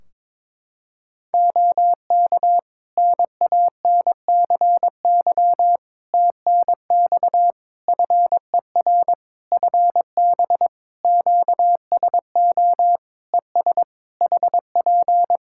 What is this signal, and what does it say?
Morse code, heard as a keyed tone, about 22 wpm, 700 Hz. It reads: OK NANCY TNX FER FB QSO ES HP
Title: Practice QSO - Exchange 4 - Ending - 22wpm